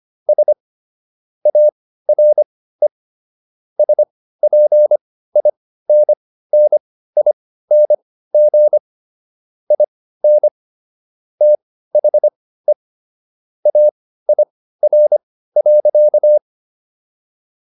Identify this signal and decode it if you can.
Morse code, heard as a keyed tone, about 25 wpm, spaced as for 15 wpm, 600 Hz. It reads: S ARE SPINNING IN THE AIR.